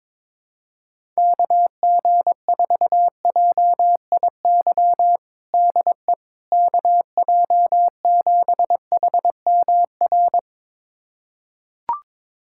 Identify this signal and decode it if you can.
Morse code, heard as a keyed tone, about 22 wpm, 700 Hz. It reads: KG4JIY DE KJ7HMR
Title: USA Callsign DE Callsign 22wpm